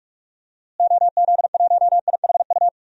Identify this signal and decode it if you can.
Morse code, heard as a keyed tone, about 46 wpm, 700 Hz. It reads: OZ1IHU
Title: DXCC-100 Call Signs - 6 Characters - Repeated 2x - 45wpm